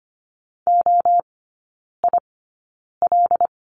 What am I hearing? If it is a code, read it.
Morse code, heard as a keyed tone, about 25 wpm, spaced as for 9 wpm, 700 Hz. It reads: OIL
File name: Lesson 14e - Number 2 - Three Characters 25wpm